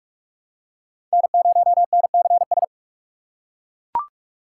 Morse code, heard as a keyed tone, about 41 wpm, 690 Hz. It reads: N0NCS
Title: USA Call Signs - 5 Characters - 6x Character Spacing - 45wpm